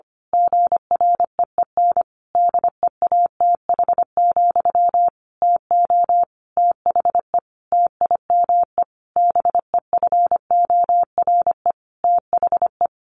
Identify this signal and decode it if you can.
Morse code, heard as a keyed tone, about 25 wpm, 700 Hz. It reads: GREEN DEATH, TO THE TIME BEFORE THE
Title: A Sound Of Thunder 25wpm